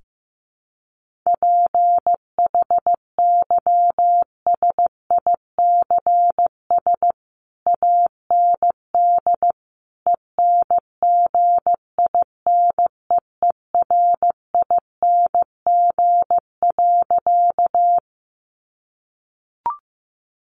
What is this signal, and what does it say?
Morse code, heard as a keyed tone, about 15 wpm, 700 Hz. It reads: PHYSICS AND ENGINEERING.
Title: Sentences from Top 1500 Words 15wpm